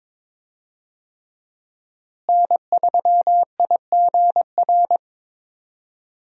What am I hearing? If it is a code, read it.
Morse code, heard as a keyed tone, about 22 wpm, 700 Hz. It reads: N3IGR
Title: USA Call Signs - 5 Characters - Repeated 3x - 22wpm